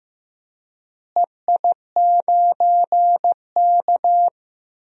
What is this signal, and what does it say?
Morse code, heard as a keyed tone, about 15 wpm, 700 Hz. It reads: EI9K
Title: DXCC-100 Call Signs - 4 Characters - Repeated 3x - 15wpm